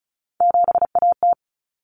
Morse code, heard as a keyed tone, about 34 wpm, 700 Hz. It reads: 7AT
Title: Common Cut Numbers - 3-Digit 35wpm